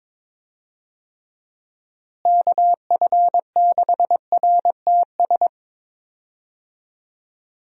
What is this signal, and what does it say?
Morse code, heard as a keyed tone, about 22 wpm, 700 Hz. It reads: KF6RTS
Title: USA Call Signs - 6 Characters - Repeated 2x - 22wpm